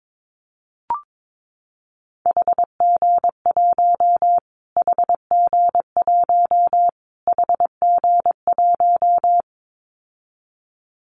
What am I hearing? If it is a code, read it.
Morse code, heard as a keyed tone, about 22 wpm, 700 Hz. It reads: HG1 HG1 HG1
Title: Callsign Prefixes for Top 25 DXCC - Repeated 3x - 22wpm